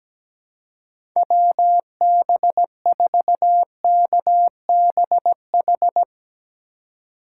Morse code, heard as a keyed tone, about 17 wpm, 700 Hz. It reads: WB4KBH
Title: USA Call Signs - 6 Characters - 3x Character Spacing - 17wpm